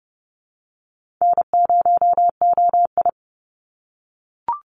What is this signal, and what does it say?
Morse code, heard as a keyed tone, about 30 wpm, 700 Hz. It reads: N0OI
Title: NA SOTA Activator Callsigns 30wpm